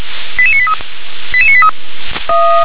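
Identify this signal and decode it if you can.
Five-tone selective call (ZVEI), heard as quick runs of five tones, tones of about 70 ms. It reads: 88983 88983
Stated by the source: ZVEI five-tone selective call 88765 received by SDR.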